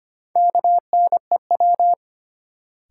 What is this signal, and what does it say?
Morse code, heard as a keyed tone, about 25 wpm, 700 Hz. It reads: KNEW
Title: Sentences from Top 400 Words - New Word Drill - 25wpm